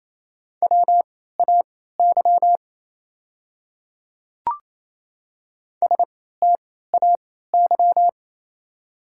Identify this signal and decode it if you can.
Morse code, heard as a keyed tone, about 28 wpm, spaced as for 16 wpm, 690 Hz. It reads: WAY STAY
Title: 3x Character Spacing - Top 500 Words - 28wpm